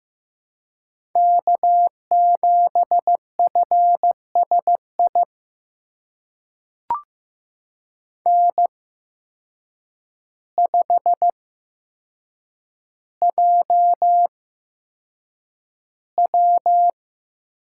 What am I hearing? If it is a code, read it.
Morse code, heard as a keyed tone, about 15 wpm, spaced as for 7 wpm, 700 Hz. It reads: K7FSI N5JW
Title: USA Call Signs - 5 Characters - 8x Character Spacing - 15wpm